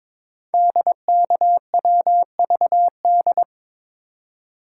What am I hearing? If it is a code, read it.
Morse code, heard as a keyed tone, about 22 wpm, 700 Hz. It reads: DKWVD
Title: Code Groups - 5 Letters - 22wpm